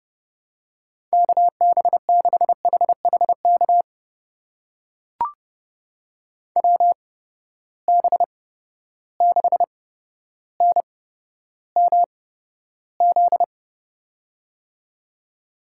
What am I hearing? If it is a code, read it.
Morse code, heard as a keyed tone, about 30 wpm, spaced as for 13 wpm, 700 Hz. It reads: KB6HHK WB6NMZ
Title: USA Call Signs - 6 Characters - 8x Character Spacing - 30wpm